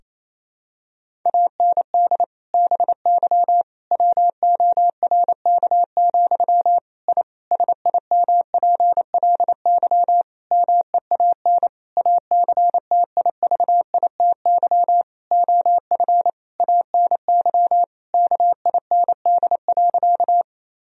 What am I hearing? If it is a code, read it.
Morse code, heard as a keyed tone, about 28 wpm, 700 Hz. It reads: AND BY WORK, I SIMPLY MEAN ACTIVITY OF ANY KIND.
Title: Sentences from Top 1000 Words - Review Part 4 - 28wpm